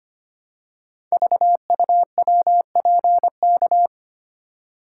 Morse code, heard as a keyed tone, about 25 wpm, 700 Hz. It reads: VUWPK
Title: Code Groups - 5 Letters - 25wpm